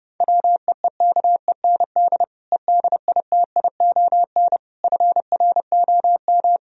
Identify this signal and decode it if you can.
Morse code, heard as a keyed tone, about 30 wpm, 700 Hz. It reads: WEEKEND EDITION FROM